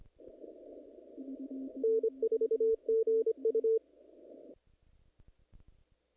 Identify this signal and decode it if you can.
Morse code, heard as a keyed tone, about 26 wpm, 450 Hz. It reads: N4GU
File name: contests / naqp-cw-k5tr-op-k5pi- -2100